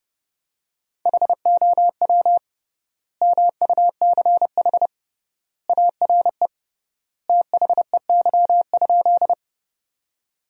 Morse code, heard as a keyed tone, about 30 wpm, 700 Hz. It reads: HOW MUCH ARE THEY?
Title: 3x Word-Spacing - Sentences from Top 200 Words - 30wpm